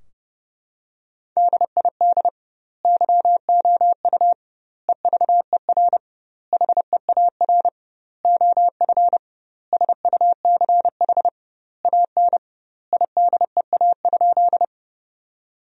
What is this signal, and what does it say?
Morse code, heard as a keyed tone, about 32 wpm, 700 Hz. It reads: DID YOU EVER HEAR OF SUCH AN IDEA?
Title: 2x Word-Spacing - Sentences from Top 400 Words - 30wpm